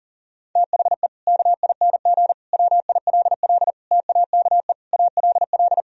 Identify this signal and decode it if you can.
Morse code, heard as a keyed tone, about 40 wpm, 700 Hz. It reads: THE KING WILL TAKE ALL
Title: Sentences from Top 1000 Words - Review Part 4 - 40wpm